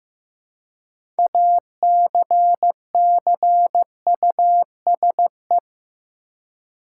Morse code, heard as a keyed tone, about 15 wpm, 700 Hz. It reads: ACCUSE
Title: Six Letter Words - 15wpm